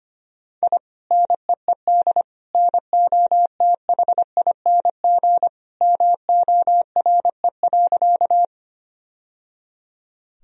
Morse code, heard as a keyed tone, about 25 wpm, 700 Hz. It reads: I NEED NOTHING MORE.
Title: Sentences from Top 1000 Words - Review Part 1 - 25wpm